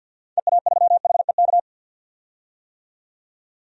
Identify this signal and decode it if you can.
Morse code, heard as a keyed tone, about 47 wpm, 700 Hz. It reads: EA3HEK
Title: DXCC-100 Call Signs - 6 Characters - 50wpm